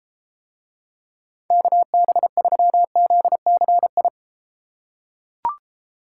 Morse code, heard as a keyed tone, about 32 wpm, 700 Hz. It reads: KB3ZCI